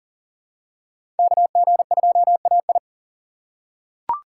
Morse code, heard as a keyed tone, about 40 wpm, 700 Hz. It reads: KG2AI
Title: USA Call Signs - 5 Characters - 6x Character Spacing - 40wpm